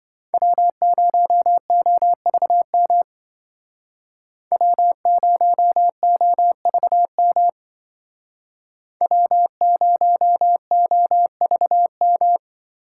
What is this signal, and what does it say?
Morse code, heard as a keyed tone, about 25 wpm, 700 Hz. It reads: W0OVM W0OVM W0OVM